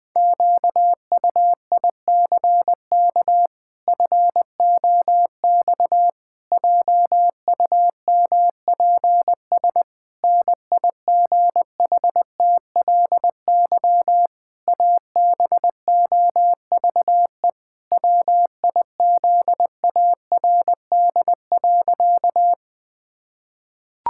Morse code, heard as a keyed tone, about 21 wpm, 700 Hz. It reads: QUICK FOX JUMPS NIGHTLY ABOVE WIZARD.
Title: Pangram Challenge v2 - 20wpm